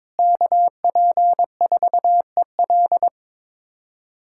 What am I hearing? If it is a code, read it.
Morse code, heard as a keyed tone, about 22 wpm, 700 Hz. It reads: KP4EL
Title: USA Callsign DE Callsign 22wpm